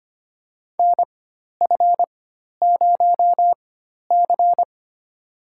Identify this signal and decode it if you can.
Morse code, heard as a keyed tone, about 25 wpm, spaced as for 12 wpm, 700 Hz. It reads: NF0C
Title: USA Call Signs - 4 Characters - 4x Character Spacing - 25wpm